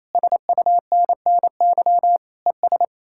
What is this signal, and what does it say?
Morse code, heard as a keyed tone, about 38 wpm, spaced as for 33 wpm, 700 Hz. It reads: SUNNY ES
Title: Speed-Racing Common QSO Phrases - 28wpm